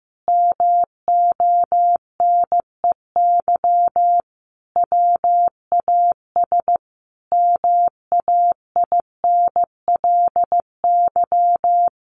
Morse code, heard as a keyed tone, about 15 wpm, 700 Hz. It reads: MONEY WAS MAINLY
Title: Sentences from Top 1400 Words 15wpm